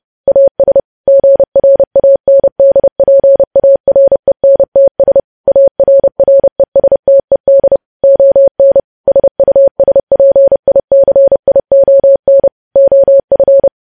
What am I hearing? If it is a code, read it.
Morse code, heard as a keyed tone, about 30 wpm, 550 Hz. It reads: AS GRANDPARENTS ARRESTED ON SUSPICION OF
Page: news headlines in morse code at 30 wpm